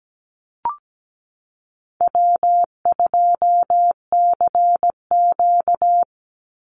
Morse code, heard as a keyed tone, about 17 wpm, 700 Hz. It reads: W2CQ
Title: Easy Call Signs Repeated 2x - 17wpm